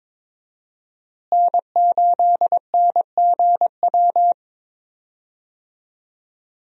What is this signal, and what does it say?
Morse code, heard as a keyed tone, about 22 wpm, 700 Hz. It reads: N8NGW
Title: USA Call Signs - 5 Characters - Repeated 2x - 22wpm